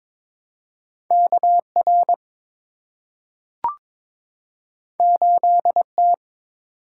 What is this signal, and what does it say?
Morse code, heard as a keyed tone, about 22 wpm, 700 Hz. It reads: KR 8T
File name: practice / 22wpm / Two Letter-Number Combinations - 22wpm